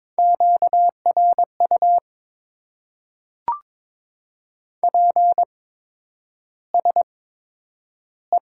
Morse code, heard as a keyed tone, about 22 wpm, spaced as for 9 wpm, 700 Hz. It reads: QRU PSE
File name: Top 90 QSO Elements - 8x Character Spacing - 22wpm